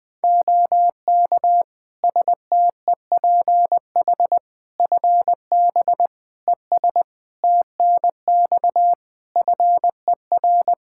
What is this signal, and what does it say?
Morse code, heard as a keyed tone, about 20 wpm, 700 Hz. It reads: OK STEPH FB ES TNX FER
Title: Practice QSO - Exchange 2 - Info - 20wpm